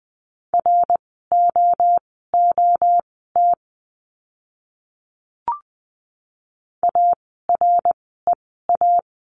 Morse code, heard as a keyed tone, about 20 wpm, spaced as for 14 wpm, 700 Hz. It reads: ROOT AREA
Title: 054 Lesson 10f - Letter r - Words 20wpm